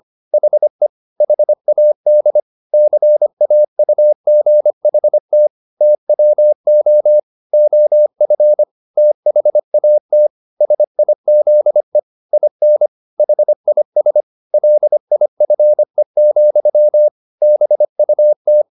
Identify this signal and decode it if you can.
Morse code, heard as a keyed tone, about 25 wpm, 600 Hz. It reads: HE HAD CAUGHT TWO OF THAT SIZE IN HIS LIFE, BUT
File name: the old man and the sea part 1 25wpm